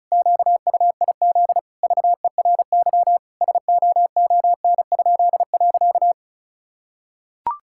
Morse code, heard as a keyed tone, about 36 wpm, 700 Hz. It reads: QUIZ VERY SOON?.
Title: Pangram Challenge v2 - 35wpm